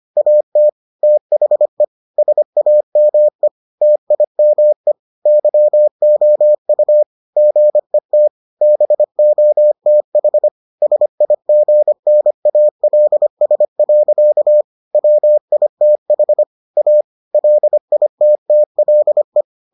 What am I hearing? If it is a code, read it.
Morse code, heard as a keyed tone, about 25 wpm, 600 Hz. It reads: AT THE SAME TIME YOU GET BOTH SIGNALS. WITH A LITTLE